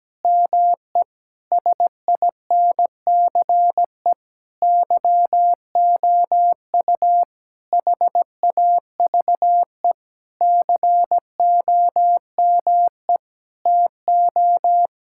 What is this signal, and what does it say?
Morse code, heard as a keyed tone, about 17 wpm, 700 Hz. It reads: ME SINCE YOU HAVE COME TO
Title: Sentences from Top 1600 Words 17wpm